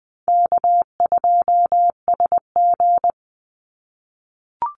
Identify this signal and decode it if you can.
Morse code, heard as a keyed tone, about 20 wpm, 700 Hz. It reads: K2SG
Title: Easy Call Signs - 20wpm